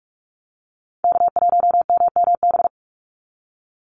Morse code, heard as a keyed tone, about 44 wpm, 700 Hz. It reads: K1MMB
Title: Call Signs Repeated 2x - 45wpm